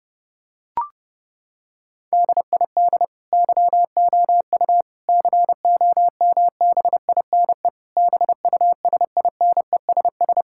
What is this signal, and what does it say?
Morse code, heard as a keyed tone, about 30 wpm, 700 Hz. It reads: DID YOU COMBINE BUSINESS
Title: Sentences from Top 1600 Words 30wpm